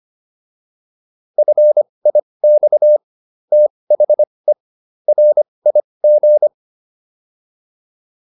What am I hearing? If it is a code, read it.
Morse code, heard as a keyed tone, about 25 wpm, spaced as for 20 wpm, 600 Hz. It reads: FIX THE RIG